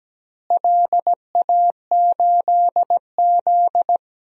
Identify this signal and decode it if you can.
Morse code, heard as a keyed tone, about 17 wpm, 700 Hz. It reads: LA8Z
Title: DXCC-100 Call Signs - 4 Characters - Repeated 2x - 17wpm